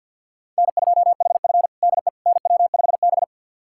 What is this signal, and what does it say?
Morse code, heard as a keyed tone, about 52 wpm, 700 Hz. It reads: N2HF DE NW5B
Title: USA Callsign DE Callsign - 4 Characters - 50wpm